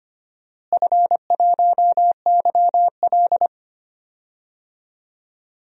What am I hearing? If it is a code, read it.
Morse code, heard as a keyed tone, about 25 wpm, 700 Hz. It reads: F1YL
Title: DXCC-100 Call Signs - 4 Characters - Repeated 3x - 25wpm